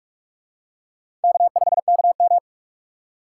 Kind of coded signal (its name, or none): Morse code